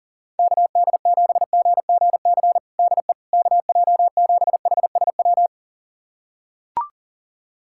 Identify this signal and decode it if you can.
Morse code, heard as a keyed tone, about 40 wpm, 700 Hz. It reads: KD7GGC DE KJ7HSW
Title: USA Callsign DE Callsign - 6 Characters - 40wpm